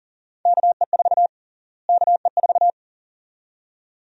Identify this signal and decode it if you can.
Morse code, heard as a keyed tone, about 40 wpm, 700 Hz. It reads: KE4 KE4
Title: USA Call Sign Prefixes - Repeated 2x - 40wpm